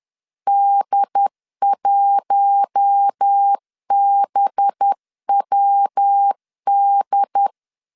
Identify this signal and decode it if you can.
Morse code, heard as a keyed tone, about 10 wpm, 800 Hz. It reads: D1BWD
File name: CW / DET / 1 6 1500C S C 10wpm 4252